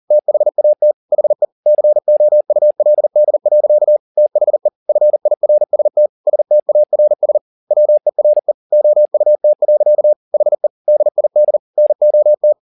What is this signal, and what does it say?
Morse code, heard as a keyed tone, about 40 wpm, 600 Hz. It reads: THAT HE COULD. THE FIRST STARS WERE OUT. HE DID NOT
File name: the old man and the sea 2 40wpm